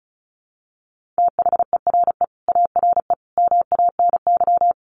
Morse code, heard as a keyed tone, about 34 wpm, 700 Hz. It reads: THERE ARE MANY
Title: Easy Phrases 35wpm